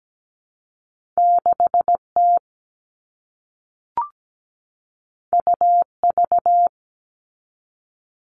Morse code, heard as a keyed tone, about 17 wpm, 700 Hz. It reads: 6T UV
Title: All Cut Numbers - 2-Digit 17wpm